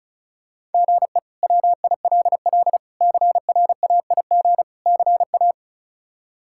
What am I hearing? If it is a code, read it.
Morse code, heard as a keyed tone, about 36 wpm, 700 Hz. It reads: GE WILL CRAIG CA
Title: K1USN SST QSO Practice - 35wpm